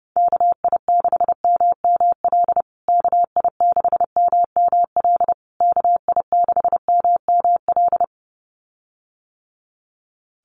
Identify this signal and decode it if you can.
Morse code, heard as a keyed tone, about 30 wpm, 700 Hz. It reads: KI6MML KI6MML KI6MML
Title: USA Call Signs Repeated 3x 30wpm